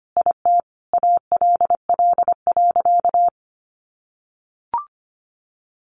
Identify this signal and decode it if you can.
Morse code, heard as a keyed tone, about 25 wpm, 700 Hz. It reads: IT ALL.
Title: Sentences from Top 1400 Words 25wpm